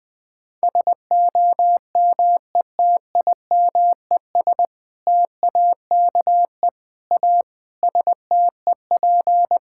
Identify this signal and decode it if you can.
Morse code, heard as a keyed tone, about 20 wpm, 700 Hz. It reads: SOMETIMES TAKE A STEP
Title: Sentences with 6-Word Phrases - 20wpm